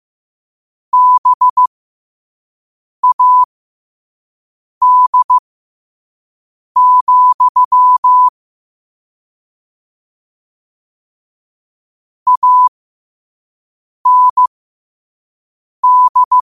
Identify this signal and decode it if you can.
Morse code, heard as a keyed tone, about 15 wpm, spaced as for 5 wpm, 1000 Hz. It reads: BAD, AND